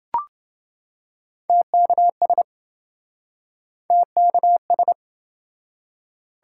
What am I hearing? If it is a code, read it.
Morse code, heard as a keyed tone, about 26 wpm, 700 Hz. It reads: TKS TKS